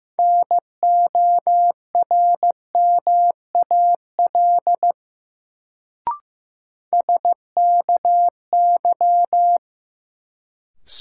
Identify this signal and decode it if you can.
Morse code, heard as a keyed tone, about 15 wpm, 700 Hz. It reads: NORMAL SKY
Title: Sentences from Top 1200 Words - New Word Drill - 15wpm